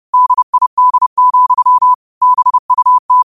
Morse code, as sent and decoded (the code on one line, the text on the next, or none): -.. .. -.. --..-- / -... ..- -
DID, BUT